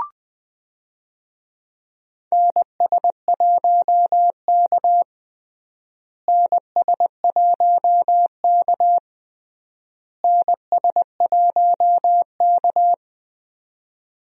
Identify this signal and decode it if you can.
Morse code, heard as a keyed tone, about 20 wpm, 700 Hz. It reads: NS1K NS1K NS1K
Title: USA Call Signs - 4 Characters - Repeated 3x - 20wpm